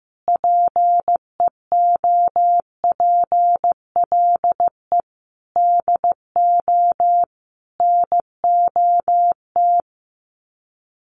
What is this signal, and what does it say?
Morse code, heard as a keyed tone, about 15 wpm, 700 Hz. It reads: PEOPLE DO NOT
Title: Sets of 5 Words Encore 15wpm